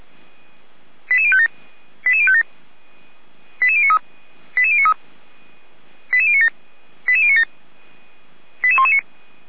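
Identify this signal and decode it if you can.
Five-tone selective call (ZVEI), heard as quick runs of five tones, tones of about 70 ms. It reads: 88057 88057 88093 88093 88097 88097 88119
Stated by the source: Five-tone ZVEI selective call received by SDR, on the audio server of feuerwehr-orlamuende.org.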